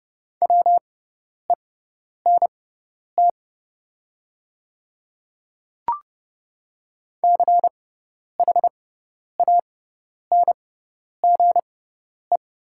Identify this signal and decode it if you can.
Morse code, heard as a keyed tone, about 30 wpm, spaced as for 10 wpm, 700 Hz. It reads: WENT CHANGE
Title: 6x Character Spacing - Top 300 Words - 30wpm